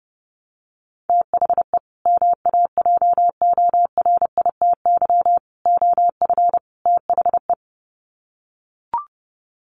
Morse code, heard as a keyed tone, about 30 wpm, 700 Hz. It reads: THE MAJORITY OF THE
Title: Sets of 4 Words 30wpm